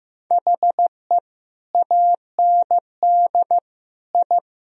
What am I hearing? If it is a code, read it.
Morse code, heard as a keyed tone, about 14 wpm, 700 Hz. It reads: HE AND I
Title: Sentences from Top 1000 Words - Review Part 1 - 15wpm